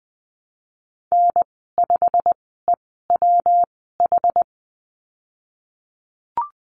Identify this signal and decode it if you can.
Morse code, heard as a keyed tone, about 20 wpm, spaced as for 15 wpm, 700 Hz. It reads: N5EWH
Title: 128 Lesson 21g - Number 6 - Call Signs 20wpm